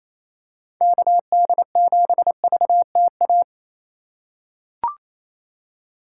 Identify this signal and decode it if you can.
Morse code, heard as a keyed tone, about 28 wpm, 700 Hz. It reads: KD7VTA
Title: USA Call Signs - 3x Character Spacing - 28wpm